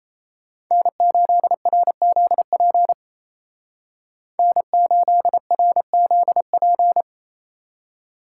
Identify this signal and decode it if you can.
Morse code, heard as a keyed tone, about 35 wpm, 700 Hz. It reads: N8RZP N8RZP
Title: Speed Racing - USA 1x3 Call Signs 28wpm